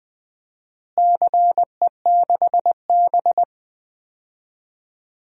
Morse code, heard as a keyed tone, about 20 wpm, 700 Hz. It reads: CE6B
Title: DXCC-100 Call Signs - 4 Characters - Repeated 2x - 20wpm